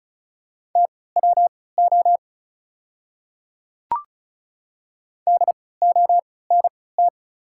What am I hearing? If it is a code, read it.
Morse code, heard as a keyed tone, about 34 wpm, spaced as for 20 wpm, 700 Hz. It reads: TWO DONT
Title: 3x Character Spacing - Top 300 Words - 35wpm